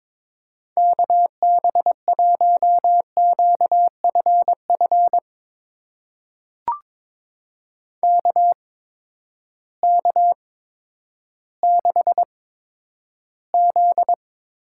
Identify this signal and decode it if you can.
Morse code, heard as a keyed tone, about 22 wpm, spaced as for 11 wpm, 700 Hz. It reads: KB1QFF KK6Z
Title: USA Call Signs - 6 Characters - 8x Character Spacing - 22wpm